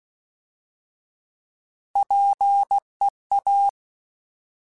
Morse code, heard as a keyed tone, about 16 wpm, 800 Hz. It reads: PEA
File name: Alphabetical / Alphabetical mp3s / Ps @15